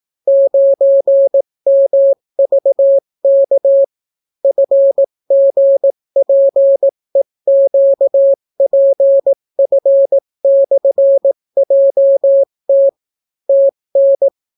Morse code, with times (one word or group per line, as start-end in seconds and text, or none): 0.3-3.8 9MVK
4.4-12.9 FGPEQPF/JT
13.5-14.3 TN